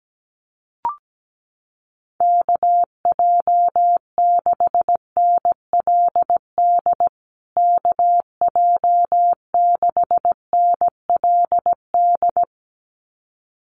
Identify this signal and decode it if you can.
Morse code, heard as a keyed tone, about 17 wpm, 700 Hz. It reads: KJ6NLD KJ6NLD
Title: USA Call Signs Repeated 2x 17wpm